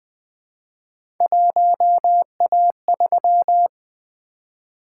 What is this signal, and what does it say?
Morse code, heard as a keyed tone, about 21 wpm, 700 Hz. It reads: 1A3
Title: Three Letter-Number Combinations - 20wpm